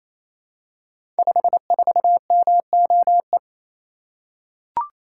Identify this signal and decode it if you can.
Morse code, heard as a keyed tone, about 27 wpm, 700 Hz. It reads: 54MOE
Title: Code Groups - 5 Letter:Numbers - 28wpm